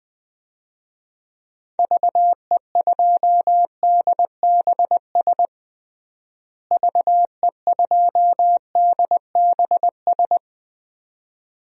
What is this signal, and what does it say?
Morse code, heard as a keyed tone, about 20 wpm, 700 Hz. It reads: VE2DBS VE2DBS
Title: DXCC-100 Call Signs - 6 Characters - Repeated 2x - 20wpm